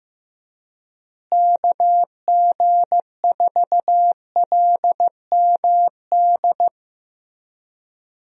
Morse code, heard as a keyed tone, about 15 wpm, 700 Hz. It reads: KG4LMD
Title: USA Call Signs - 6 Characters - Repeated 2x - 15wpm